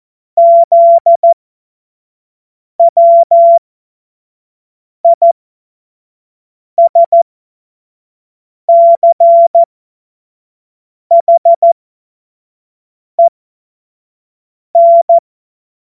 Morse code, heard as a keyed tone, about 14 wpm, spaced as for 5 wpm, 680 Hz. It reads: ZWISCHEN
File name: Morsekurs / Witze / Witz 06. Januar 2026 25 F70